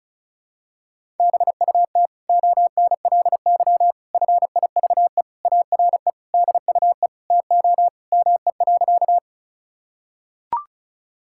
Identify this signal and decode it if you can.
Morse code, heard as a keyed tone, about 34 wpm, 700 Hz. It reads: BUT ONLY FIVE ARE DUE TO ME.
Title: Sentences from Top 1000 Words - Review Part 6 - 35wpm